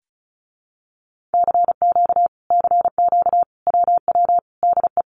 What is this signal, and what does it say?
Morse code, heard as a keyed tone, about 34 wpm, 700 Hz. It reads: CQ CQ WW DE
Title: Common QSO Phrases Encore 35wpm